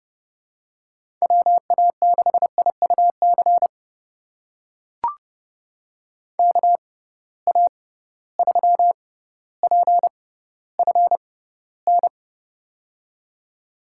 Morse code, heard as a keyed tone, about 32 wpm, spaced as for 16 wpm, 700 Hz. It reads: WA6IUC KA3PFN
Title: USA Call Signs - 6 Characters - 6x Character Spacing - 30wpm